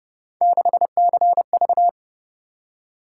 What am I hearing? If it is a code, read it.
Morse code, heard as a keyed tone, about 29 wpm, 700 Hz. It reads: 6CV